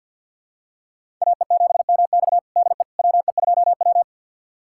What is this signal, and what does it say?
Morse code, heard as a keyed tone, about 47 wpm, 700 Hz. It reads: AE7MX DE WE2W
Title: USA Callsign DE Callsign 50wpm